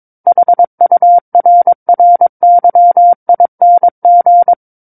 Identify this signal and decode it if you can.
Morse code, heard as a keyed tone, about 22 wpm, 700 Hz. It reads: HURRYING
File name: Book One / 13-Chapter 13 - 22 WPM